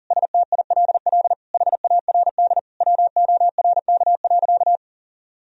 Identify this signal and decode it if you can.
Morse code, heard as a keyed tone, about 40 wpm, 700 Hz. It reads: STILL HARD WORK.
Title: Sentences from Top 1000 Words - Review Part 3 - 40wpm